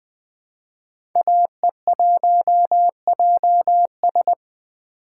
Morse code, heard as a keyed tone, about 20 wpm, 700 Hz. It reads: AE1JS
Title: 2023 SOTA Chaser Call Signs - 20wpm